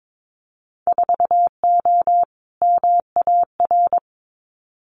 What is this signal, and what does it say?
Morse code, heard as a keyed tone, about 22 wpm, 700 Hz. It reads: 4O MAR
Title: Winter Field Day Exchanges Encore 22wpm